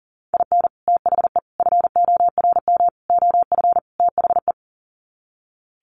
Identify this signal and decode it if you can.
Morse code, heard as a keyed tone, about 40 wpm, 700 Hz. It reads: IN THE FORM OF THE
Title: Sets of 5 Words Encore2 40wpm